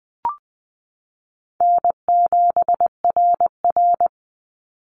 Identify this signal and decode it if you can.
Morse code, heard as a keyed tone, about 20 wpm, 700 Hz. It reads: N7RR
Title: Call Signs Repeated 2x - 20wpm